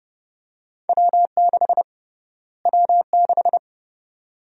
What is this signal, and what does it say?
Morse code, heard as a keyed tone, about 30 wpm, 700 Hz. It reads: W6 W6
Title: USA Call Sign Prefixes - Repeated 2x - 30wpm